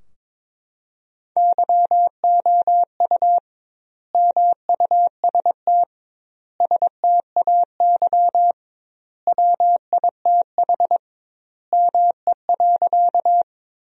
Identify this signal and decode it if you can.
Morse code, heard as a keyed tone, about 22 wpm, 700 Hz. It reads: YOU MUST STAY WITH ME.
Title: 2x Word-Spacing - Sentences from Top 500 Words - 22wpm